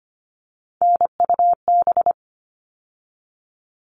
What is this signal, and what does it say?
Morse code, heard as a keyed tone, about 25 wpm, 700 Hz. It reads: NUB
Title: All Cut Numbers - 3-Digit 25wpm